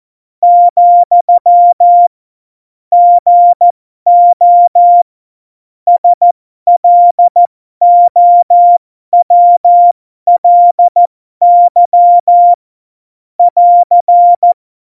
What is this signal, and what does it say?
Morse code, heard as a keyed tone, about 14 wpm, spaced as for 12 wpm, 700 Hz. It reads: , GO SLOWLY <AR>
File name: Morsekurs / Witze